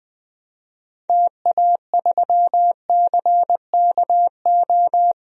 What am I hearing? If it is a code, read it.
Morse code, heard as a keyed tone, about 20 wpm, 700 Hz. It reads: TA3CKO
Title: DXCC-100 Call Signs - 6 Characters - 20wpm